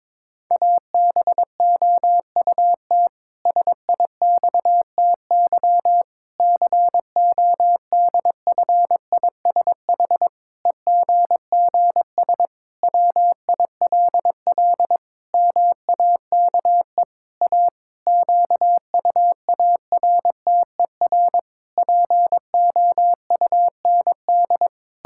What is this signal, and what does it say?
Morse code, heard as a keyed tone, about 22 wpm, 700 Hz. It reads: ABOUT SIXTY CODFISH EGGS WILL MAKE A QUARTER POUND
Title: Pangram Challenge v2 - 22wpm